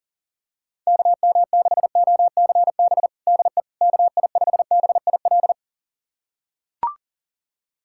Morse code, heard as a keyed tone, about 40 wpm, 700 Hz. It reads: KM6OCB DE KI5BIL
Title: USA Callsign DE Callsign - 6 Characters - 40wpm